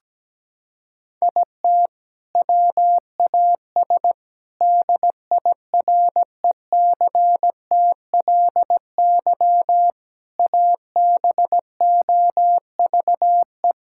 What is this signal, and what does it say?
Morse code, heard as a keyed tone, about 17 wpm, 700 Hz. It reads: IT WAS DIRECTLY ABOVE
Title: Sentences from Top 1000 Words - Review Part 5 - 17wpm